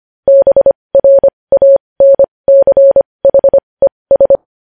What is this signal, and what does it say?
Morse code, heard as a keyed tone, about 25 wpm, spaced as for 20 wpm, 550 Hz. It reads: BRANCHES
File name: podcast / News headlines-1x-25x20